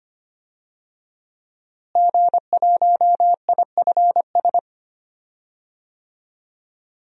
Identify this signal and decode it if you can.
Morse code, heard as a keyed tone, about 25 wpm, 700 Hz. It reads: G1IFS
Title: DXCC-100 Call Signs - 5 Characters - Repeated 3x - 25wpm